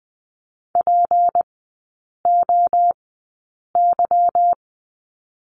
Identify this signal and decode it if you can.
Morse code, heard as a keyed tone, about 20 wpm, spaced as for 8 wpm, 700 Hz. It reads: POY
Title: 152 Lesson 25e - Letter p - Three Characters 20wpm